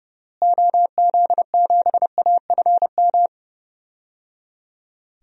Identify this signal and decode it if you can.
Morse code, heard as a keyed tone, about 30 wpm, 700 Hz. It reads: OZ7AFM
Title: DXCC-100 Call Signs - 6 Characters - Repeated 3x - 30wpm